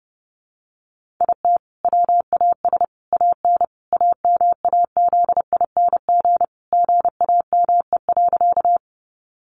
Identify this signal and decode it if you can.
Morse code, heard as a keyed tone, about 30 wpm, 700 Hz. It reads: IT WAS AN AMAZING GAME.
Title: Sentences from Top 1500 Words 30wpm